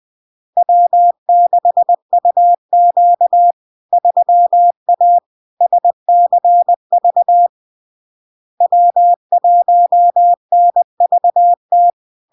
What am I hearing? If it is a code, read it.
Morse code, heard as a keyed tone, about 20 wpm, 700 Hz. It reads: W6UQ 3A SCV W1NVT
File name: LARK-FD-2022-Log-in-CW-20wpm-700-Hz